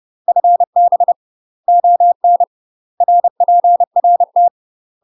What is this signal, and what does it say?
Morse code, heard as a keyed tone, about 30 wpm, 700 Hz. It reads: FB ON RPRT
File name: UK MASTER QSOs TRAINING / 30WPM / UK MASTER QSOs 0016